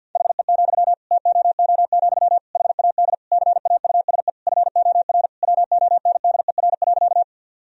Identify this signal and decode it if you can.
Morse code, heard as a keyed tone, about 50 wpm, 700 Hz. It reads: HE, TOO, HAD CAUSE FOR WONDER.
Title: Sentences from Top 1000 Words - Review Part 3 - 50wpm